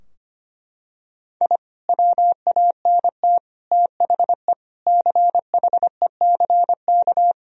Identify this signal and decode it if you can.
Morse code, heard as a keyed tone, about 25 wpm, 700 Hz. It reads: I WANT THE CHECK
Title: Sentences from Top 1000 Words - Review Part 3 - 25wpm